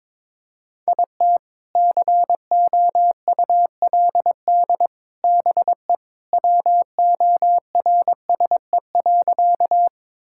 Morse code, heard as a keyed tone, about 21 wpm, 700 Hz. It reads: IT COULD BE WORSE.
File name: practice / 22wpm / Sentences with 3-Word Phrases - 22wpm